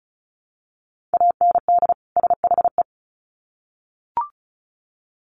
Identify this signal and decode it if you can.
Morse code, heard as a keyed tone, about 34 wpm, 700 Hz. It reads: AND SHE
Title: Sets of 2 Words 35wpm